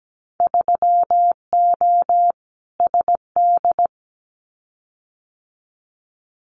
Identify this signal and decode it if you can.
Morse code, heard as a keyed tone, about 17 wpm, 700 Hz. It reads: 3O SD
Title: Winter Field Day Exchanges 17wpm